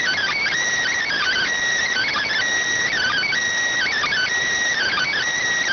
Emergency Alert System (SAME) tones, heard as harsh buzzing bursts. It is heard from 1.0 to 5.1 s.